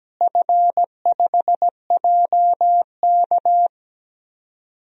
Morse code, heard as a keyed tone, about 17 wpm, 700 Hz. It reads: F5JK
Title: DXCC-100 Call Signs - 4 Characters - Repeated 3x - 17wpm